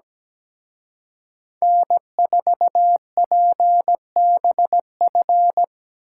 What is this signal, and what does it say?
Morse code, heard as a keyed tone, about 20 wpm, 700 Hz. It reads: N4PBF
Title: Speed Racing - USA 1x3 Call Signs 17wpm